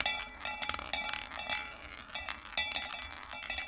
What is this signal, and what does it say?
There is no code.